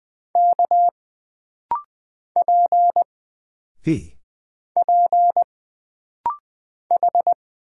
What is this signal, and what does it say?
Morse code, heard as a keyed tone, about 20 wpm, 700 Hz. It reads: K P P H
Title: Single Character (Letter:Number:Punctuation:Prosign) - Rapid-Fire - 20wpm